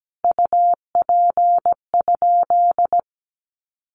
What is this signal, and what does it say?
Morse code, heard as a keyed tone, about 17 wpm, 700 Hz. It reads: UP?
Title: Sentences from Top 1400 Words 17wpm